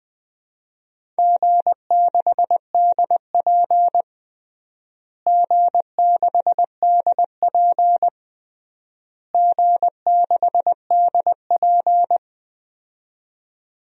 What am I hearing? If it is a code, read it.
Morse code, heard as a keyed tone, about 20 wpm, 700 Hz. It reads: G6DP G6DP G6DP
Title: DXCC-100 Call Signs - 4 Characters - Repeated 3x - 20wpm